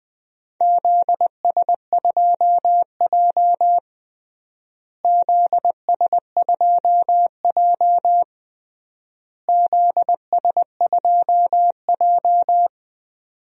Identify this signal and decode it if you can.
Morse code, heard as a keyed tone, about 20 wpm, 700 Hz. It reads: ZS2J ZS2J ZS2J
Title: DXCC-100 Call Signs - 4 Characters - Repeated 3x - 20wpm